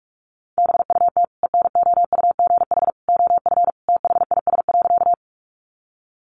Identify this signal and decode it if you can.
Morse code, heard as a keyed tone, about 44 wpm, 700 Hz. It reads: BUT ENOUGH OF THIS.
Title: Sentences from Top 400 Words 45wpm